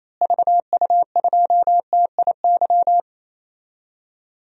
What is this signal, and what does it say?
Morse code, heard as a keyed tone, about 28 wpm, 700 Hz. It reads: VU2TIY
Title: DXCC-100 Call Signs - 6 Characters - Repeated 3x - 28wpm